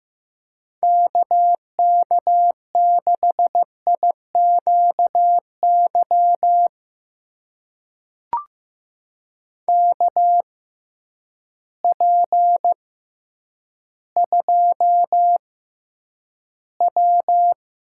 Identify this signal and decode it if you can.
Morse code, heard as a keyed tone, about 15 wpm, spaced as for 9 wpm, 700 Hz. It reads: KK6IQY KP2W
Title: USA Call Signs - 6x Character Spacing - 15wpm